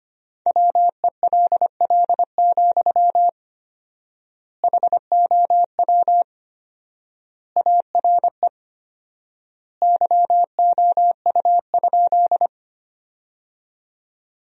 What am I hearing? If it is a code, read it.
Morse code, heard as a keyed tone, about 25 wpm, 700 Hz. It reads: WELL, HOW ARE YOU?
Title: 4x Word-Spacing - Sentences from Top 200 Words - 25wpm